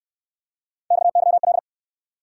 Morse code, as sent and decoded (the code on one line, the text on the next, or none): -..- -.-- .-..
XYL